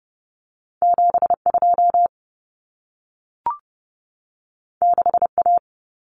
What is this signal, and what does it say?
Morse code, heard as a keyed tone, about 30 wpm, 700 Hz. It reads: 72 6A
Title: Common Cut Numbers - 2-Digit 30wpm